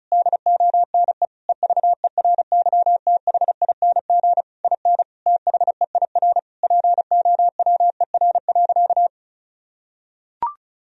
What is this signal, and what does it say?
Morse code, heard as a keyed tone, about 34 wpm, 700 Hz. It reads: DONE EVERYTHING IN THEIR POWER.
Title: Sentences from Top 1000 Words - Review Part 2 - 35wpm